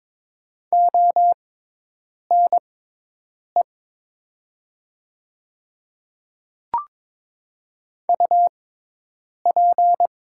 Morse code, heard as a keyed tone, about 22 wpm, spaced as for 8 wpm, 700 Hz. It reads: ONE UP
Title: 6x Character Spacing - Top 100 Words - 22wpm